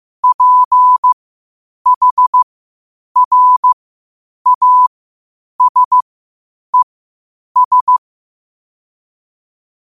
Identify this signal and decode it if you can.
Morse code, heard as a keyed tone, about 15 wpm, spaced as for 8 wpm, 1000 Hz. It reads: PHRASES
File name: qotdfiles / 8wpm